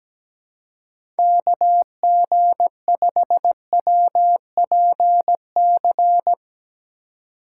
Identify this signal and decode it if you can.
Morse code, heard as a keyed tone, about 17 wpm, 700 Hz. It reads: KG5WPC
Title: USA Call Signs - 4x Character Spacing - 17wpm